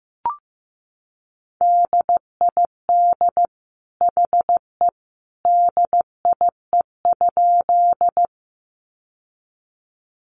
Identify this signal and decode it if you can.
Morse code, heard as a keyed tone, about 15 wpm, 700 Hz. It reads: DID HE DIE?
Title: Sentences from Top 1300 Words 15wpm